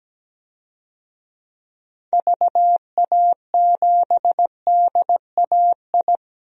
Morse code, heard as a keyed tone, about 17 wpm, 700 Hz. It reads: VA7DAI
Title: DXCC-100 Call Signs - 6 Characters - Repeated 3x - 17wpm